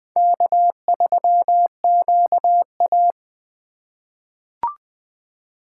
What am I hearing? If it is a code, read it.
Morse code, heard as a keyed tone, about 20 wpm, 700 Hz. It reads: K3QA
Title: USA Call Signs - 4 Characters - 2x Character Spacing - 20wpm